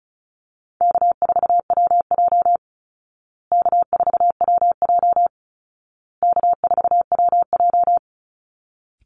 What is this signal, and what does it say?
Morse code, heard as a keyed tone, about 34 wpm, 700 Hz. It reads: K4WJ K4WJ K4WJ
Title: Easy Call Signs Repeated 3x - 35wpm